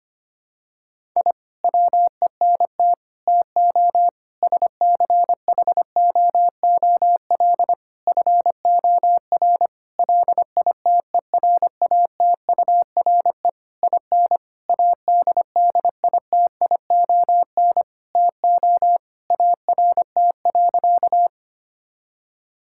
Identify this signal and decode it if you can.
Morse code, heard as a keyed tone, about 25 wpm, 700 Hz. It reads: I WENT TO SCHOOL FOR LITERATURE IN ADDITION TO ART.
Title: Sentences from Top 1000 Words - Review Part 6 - 25wpm